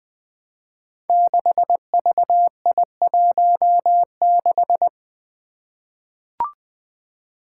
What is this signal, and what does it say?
Morse code, heard as a keyed tone, about 20 wpm, 700 Hz. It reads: 6VI16
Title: Five Challenging Letter-Number Combinations 20wpm